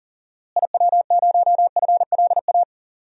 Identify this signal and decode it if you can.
Morse code, heard as a keyed tone, about 37 wpm, 700 Hz. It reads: IW0FLA
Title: DXCC-100 Call Signs - 6 Characters - Repeated 3x - 40wpm